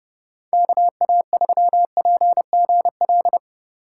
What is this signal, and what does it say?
Morse code, heard as a keyed tone, about 32 wpm, 700 Hz. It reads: KA3PGL
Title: USA Call Signs - 6 Characters - 4x Character Spacing - 30wpm